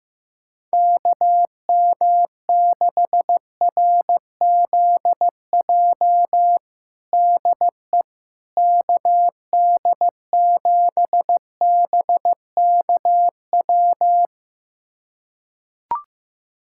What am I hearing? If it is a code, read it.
Morse code, heard as a keyed tone, about 15 wpm, 700 Hz. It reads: KM6RZJ DE KD7BKW
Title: USA Callsign DE Callsign - 6 Characters - 15wpm